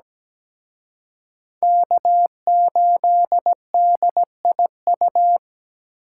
Morse code, heard as a keyed tone, about 20 wpm, 700 Hz. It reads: K8DIU
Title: Speed Racing - USA 1x3 Call Signs 17wpm